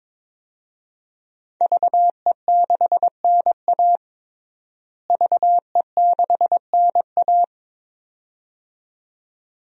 Morse code, heard as a keyed tone, about 22 wpm, 700 Hz. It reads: VE6NA VE6NA
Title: DXCC-100 Call Signs - 5 Characters - Repeated 2x - 22wpm